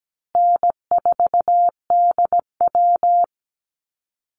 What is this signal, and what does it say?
Morse code, heard as a keyed tone, about 17 wpm, 700 Hz. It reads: N4DW
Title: Easy Call Signs Repeated 2x - 17wpm